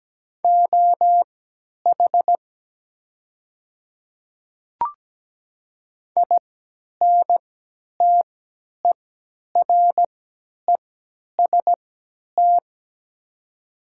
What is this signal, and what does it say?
Morse code, heard as a keyed tone, about 17 wpm, spaced as for 10 wpm, 700 Hz. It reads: OH INTEREST
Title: 3x Character Spacing - Top 500 Words - 17wpm